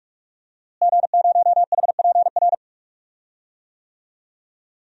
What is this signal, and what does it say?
Morse code, heard as a keyed tone, about 46 wpm, 700 Hz. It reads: G0HPR
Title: DXCC-100 Call Signs - 5 Characters - Repeated 3x - 45wpm